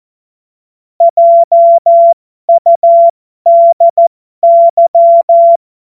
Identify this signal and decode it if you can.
Morse code, heard as a keyed tone, about 14 wpm, spaced as for 12 wpm, 680 Hz. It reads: JUDY
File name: Morsekurs / Witze